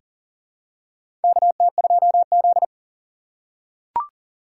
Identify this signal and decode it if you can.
Morse code, heard as a keyed tone, about 37 wpm, 700 Hz. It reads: KT2Z
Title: USA Call Signs - 3x Character Spacing - 40wpm